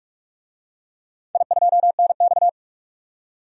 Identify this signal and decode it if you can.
Morse code, heard as a keyed tone, about 41 wpm, 700 Hz. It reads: I2NX
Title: DXCC-100 Call Signs - 4 Characters - 45wpm